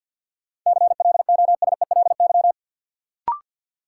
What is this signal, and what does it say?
Morse code, heard as a keyed tone, about 53 wpm, 700 Hz. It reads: CLOSELY
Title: Sentences from Top 1400 Words - New Word Drill - 50wpm